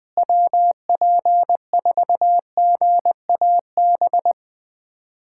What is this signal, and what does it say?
Morse code, heard as a keyed tone, about 20 wpm, 700 Hz. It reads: WP4GAB
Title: USA Callsign DE Callsign - 6 Characters - 20wpm